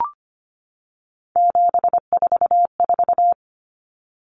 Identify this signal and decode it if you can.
Morse code, heard as a keyed tone, about 25 wpm, 700 Hz. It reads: 744
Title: Numbers - 3-Digit 25wpm